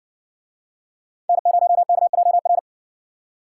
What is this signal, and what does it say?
Morse code, heard as a keyed tone, about 60 wpm, 700 Hz. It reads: N0KJR